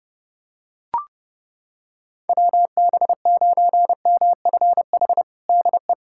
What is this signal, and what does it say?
Morse code, heard as a keyed tone, about 32 wpm, 700 Hz. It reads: WB9MFH DE
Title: USA Callsign DE Callsign - 6 Characters - 30wpm